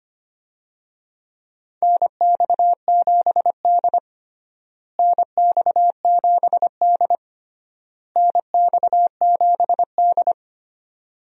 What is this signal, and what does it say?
Morse code, heard as a keyed tone, about 25 wpm, 700 Hz. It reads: NX7D NX7D NX7D
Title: USA Call Signs - 4 Characters - Repeated 3x - 25wpm